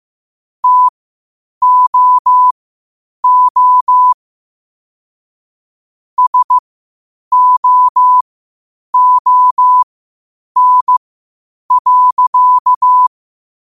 Morse code, heard as a keyed tone, about 15 wpm, spaced as for 8 wpm, 1000 Hz. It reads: TOO SOON.
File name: qotdfiles / 8wpm